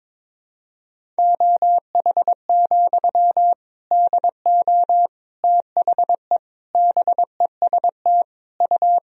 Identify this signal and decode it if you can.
Morse code, heard as a keyed tone, about 21 wpm, 700 Hz. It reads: OH, DO THE BEST U
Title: Sentences from Top 500 Words and Abbreviations 22wpm